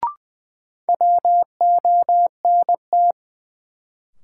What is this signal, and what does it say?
Morse code, heard as a keyed tone, about 20 wpm, 700 Hz. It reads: WONT
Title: Sentences from Top 1000 Words - Review Part 6 - New Word Drill - 20wpm